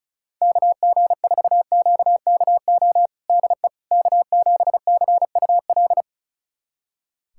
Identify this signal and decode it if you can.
Morse code, heard as a keyed tone, about 34 wpm, 700 Hz. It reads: KG4QKO DE K7CUL
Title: USA Callsign DE Callsign 35wpm